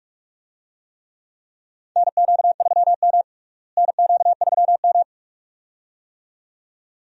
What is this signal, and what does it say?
Morse code, heard as a keyed tone, about 46 wpm, 700 Hz. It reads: NQ3M NQ3M
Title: USA Call Signs - 4 Characters - Repeated 2x - 45wpm